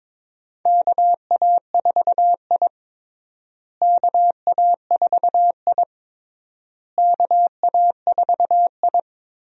Morse code, heard as a keyed tone, about 22 wpm, 700 Hz. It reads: KA4I KA4I KA4I
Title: USA Call Signs - 4 Characters - Repeated 3x - 22wpm